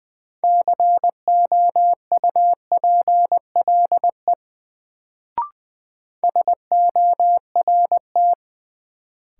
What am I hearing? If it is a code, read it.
Morse code, heard as a keyed tone, about 20 wpm, 700 Hz. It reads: COUPLE SORT
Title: Sets of 2 Words - New Word Drill - 20wpm